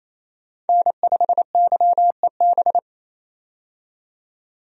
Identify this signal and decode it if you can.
Morse code, heard as a keyed tone, about 28 wpm, 700 Hz. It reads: N5YEB
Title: USA Call Signs - 5 Characters - 28wpm